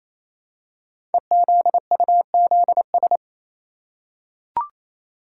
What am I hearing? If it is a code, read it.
Morse code, heard as a keyed tone, about 27 wpm, 690 Hz. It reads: EZUZS
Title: Code Groups - 5 Letters - 28wpm